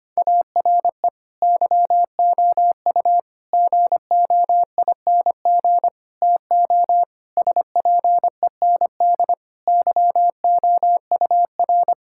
Morse code, heard as a keyed tone, about 25 wpm, 700 Hz. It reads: ARE YOU GOING TO SPEND YOUR
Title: Sentences from Top 1000 Words - Review Part 5 - 25wpm